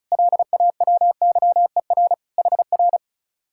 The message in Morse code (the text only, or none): LAWYER HR